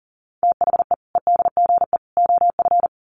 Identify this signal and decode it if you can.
Morse code, heard as a keyed tone, about 40 wpm, 700 Hz. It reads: THE EDGE OF
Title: Sets of 3 Words 40wpm